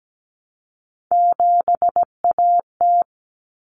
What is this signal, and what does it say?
Morse code, heard as a keyed tone, about 17 wpm, 700 Hz. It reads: 7AT
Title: Common Cut Numbers - 3-Digit 17wpm